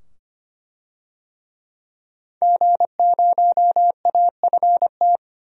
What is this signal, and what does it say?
Morse code, heard as a keyed tone, about 25 wpm, 700 Hz. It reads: G0AFT
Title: DXCC-100 Call Signs - 5 Characters - 25wpm